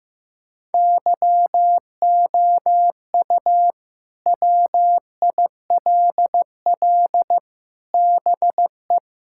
Morse code, heard as a keyed tone, about 15 wpm, 700 Hz. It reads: YOU WILL BE
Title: Sentences from Top 1600 Words 15wpm